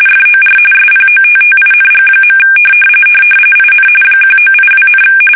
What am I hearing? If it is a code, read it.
RTTY, about 45.45 baud. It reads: FOR SPOTTING. IN 1982,
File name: fsk850 / battleshipnj